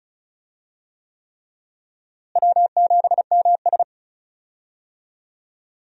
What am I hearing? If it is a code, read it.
Morse code, heard as a keyed tone, about 36 wpm, 700 Hz. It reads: W7MS
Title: USA Call Signs - 4 Characters - Repeated 3x - 35wpm